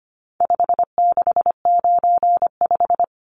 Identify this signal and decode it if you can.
Morse code, heard as a keyed tone, about 25 wpm, 700 Hz. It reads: 5695
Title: Numbers - 4-Digit 25wpm